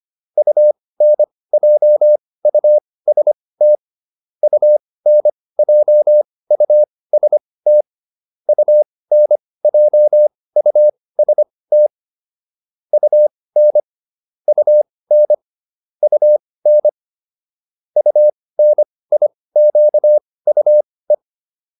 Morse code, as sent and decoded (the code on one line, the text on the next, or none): ..- -. .--- ..- ... - / ..- -. .--- ..- ... - / ..- -. .--- ..- ... - / ..- -. / ..- -. / ..- -. / ..- -. .. --.- ..- .
UNJUST UNJUST UNJUST UN UN UN UNIQUE